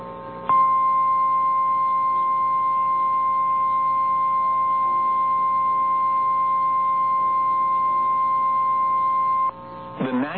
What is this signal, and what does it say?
Emergency Alert System (SAME) tones (the 1050 Hz attention tone). It is heard from 0.4 to 9.6 s.
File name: eas / KNCYKBIE / KNCYKBIE